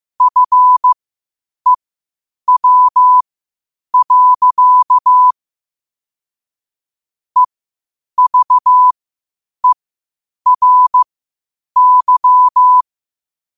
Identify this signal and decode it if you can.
Morse code, heard as a keyed tone, about 15 wpm, spaced as for 8 wpm, 1000 Hz. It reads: FEW. EVERY
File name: qotdfiles / 8wpm